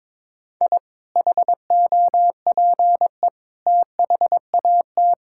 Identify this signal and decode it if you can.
Morse code, heard as a keyed tone, about 22 wpm, 700 Hz. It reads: I HOPE THAT
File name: practice / 22wpm / Sentences with 3-Word Phrases - 22wpm